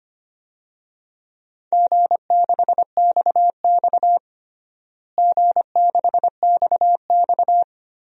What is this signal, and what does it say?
Morse code, heard as a keyed tone, about 25 wpm, 700 Hz. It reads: G6XX G6XX
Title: DXCC-100 Call Signs - 4 Characters - Repeated 2x - 25wpm